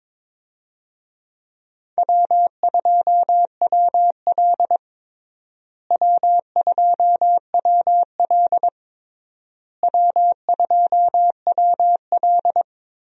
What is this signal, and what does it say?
Morse code, heard as a keyed tone, about 22 wpm, 700 Hz. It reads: W2WL W2WL W2WL
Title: USA Call Signs - 4 Characters - Repeated 3x - 22wpm